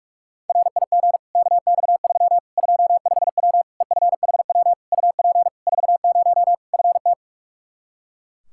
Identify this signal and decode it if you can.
Morse code, heard as a keyed tone, about 46 wpm, 700 Hz. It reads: RIG KX3 25W EFHW UP 40 FT
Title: Practice QSO - Exchange 2 - Info - 45wpm